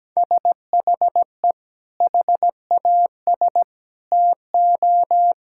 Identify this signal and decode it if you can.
Morse code, heard as a keyed tone, about 17 wpm, 700 Hz. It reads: SHE HAS TO
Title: Sentences from Top 1600 Words 17wpm